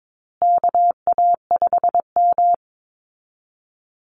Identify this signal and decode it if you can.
Morse code, heard as a keyed tone, about 22 wpm, 700 Hz. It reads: KA5M
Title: Call Signs Repeated 2x - 22wpm